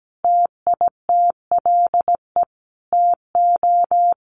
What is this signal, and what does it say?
Morse code, heard as a keyed tone, about 17 wpm, 700 Hz. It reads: TITLE TO
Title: Sentences from Top 1200 Words 17wpm